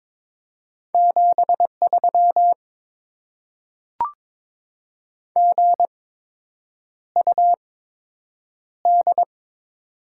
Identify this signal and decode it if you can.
Morse code, heard as a keyed tone, about 22 wpm, spaced as for 8 wpm, 700 Hz. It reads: 73 GUD
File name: Top 20 QSO Elements - 8x Character Spacing - 22wpm